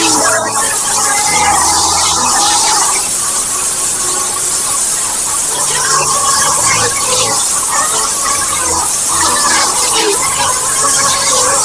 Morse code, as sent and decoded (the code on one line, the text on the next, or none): none